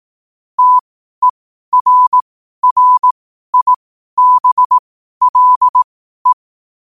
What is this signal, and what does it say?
Morse code, heard as a keyed tone, about 18 wpm, spaced as for 12 wpm, 1000 Hz. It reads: TERRIBLE
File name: qotdfiles / 12wpm